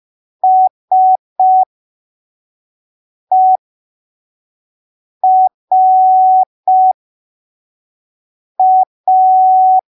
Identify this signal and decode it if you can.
Morse code, heard as a keyed tone, about 5 wpm, spaced as for 3 wpm, 750 Hz. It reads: SERA